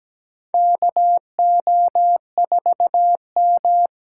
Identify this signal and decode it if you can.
Morse code, heard as a keyed tone, about 17 wpm, 700 Hz. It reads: KO4M
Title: USA Call Signs - 6x Character Spacing - 17wpm